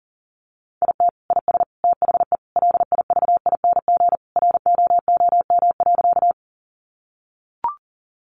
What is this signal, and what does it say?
Morse code, heard as a keyed tone, about 40 wpm, 700 Hz. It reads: IT IS THE LIVING ROOM.
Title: Sentences from Top 1100 Words 40wpm